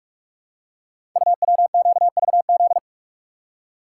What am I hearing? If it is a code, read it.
Morse code, heard as a keyed tone, about 44 wpm, 690 Hz. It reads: UWQVZ
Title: Code Groups - 5 Letters - 45wpm